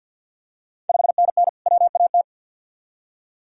Morse code, heard as a keyed tone, about 52 wpm, 690 Hz. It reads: 5NN WAT
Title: Ontario QSO Party Exchange 50wpm